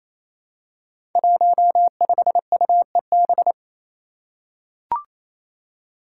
Morse code, heard as a keyed tone, about 28 wpm, 700 Hz. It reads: 15UEB
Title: Five Challenging Letter-Number Combinations 28wpm